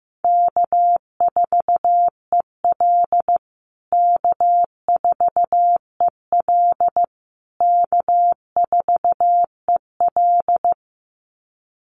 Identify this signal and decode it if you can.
Morse code, heard as a keyed tone, about 15 wpm, 700 Hz. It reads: K4EL K4EL K4EL
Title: USA Call Signs Repeated 3x 15wpm